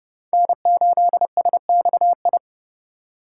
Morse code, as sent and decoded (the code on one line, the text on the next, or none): -. ---.. ... -..- ..
N8SXI